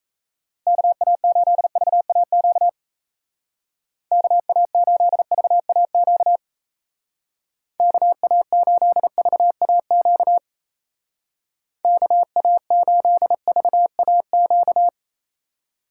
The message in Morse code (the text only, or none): KA8VAQ KA8VAQ KA8VAQ KA8VAQ